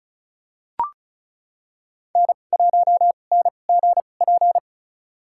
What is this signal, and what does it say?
Morse code, heard as a keyed tone, about 34 wpm, spaced as for 25 wpm, 700 Hz. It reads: N1NGP
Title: USA Call Signs - 2x Character Spacing - 35wpm